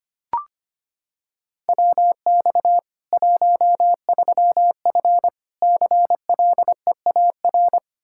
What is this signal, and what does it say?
Morse code, heard as a keyed tone, about 25 wpm, 700 Hz. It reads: WX 13F CLEAR
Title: K1USN SST QSO Practice 25wpm